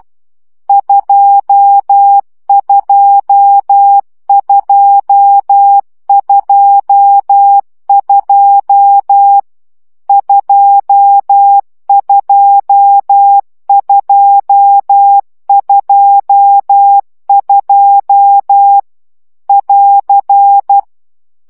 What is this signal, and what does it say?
Morse code, heard as a keyed tone, about 12 wpm, 800 Hz. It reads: 22222 22222 <AR>
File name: CW gyakorlas 2 10 baud